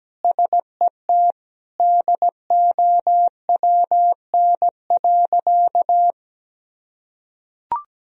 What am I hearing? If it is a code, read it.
Morse code, heard as a keyed tone, about 17 wpm, 700 Hz. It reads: SET DOWN.
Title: Sentences from Top 1000 Words - Review Part 1 - 17wpm